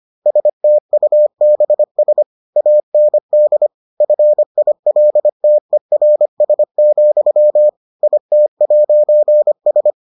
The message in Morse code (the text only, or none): STUBS AND FILTERS, IT'S